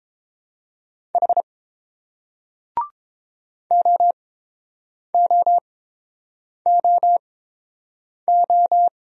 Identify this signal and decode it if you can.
Morse code, heard as a keyed tone, about 30 wpm, spaced as for 7 wpm, 690 Hz. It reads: H OOOO